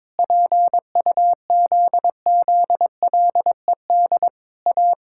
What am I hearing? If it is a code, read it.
Morse code, heard as a keyed tone, about 22 wpm, 700 Hz. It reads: PUZZLED A
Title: Pangram Challenge v2 - 22wpm